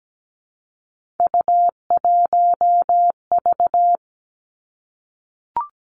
Morse code, heard as a keyed tone, about 17 wpm, 700 Hz. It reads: U1V
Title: Three Challenging Letter-Number Combinations 17wpm